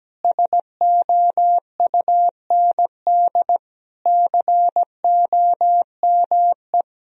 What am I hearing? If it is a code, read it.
Morse code, heard as a keyed tone, about 17 wpm, 700 Hz. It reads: SOUND COME
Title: Sentences from Top 1000 Words - Review Part 1 - 17wpm